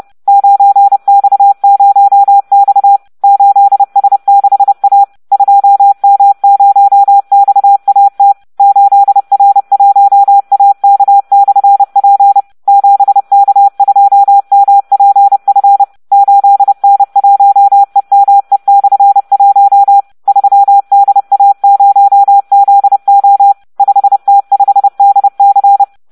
Morse code, heard as a keyed tone, about 30 wpm, 800 Hz. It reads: 9X0X 8S6A 2M0XAT 8R1AK/P 7K2MPF 8N1EME/1 3DA0ZO 5T5DC